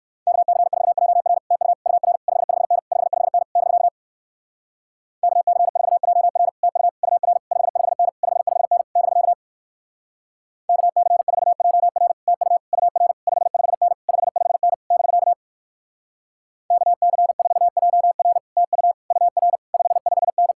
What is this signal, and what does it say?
Morse code, heard as a keyed tone, about 59 wpm, 700 Hz. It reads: KC4JR TU UR 55N 55N <BK> KC4JR TU UR 55N 55N <BK> KC4JR TU UR 55N 55N <BK> KC4JR TU UR 55N